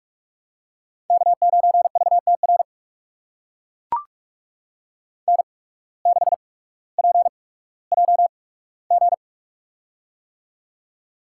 Morse code, heard as a keyed tone, about 44 wpm, spaced as for 19 wpm, 700 Hz. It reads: K9VTR N6PJG
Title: USA Call Signs - 5 Characters - 8x Character Spacing - 45wpm